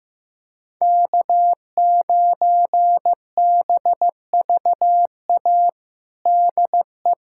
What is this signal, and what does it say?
Morse code, heard as a keyed tone, about 16 wpm, 700 Hz. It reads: K9BVA DE
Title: USA Callsign DE Callsign 15wpm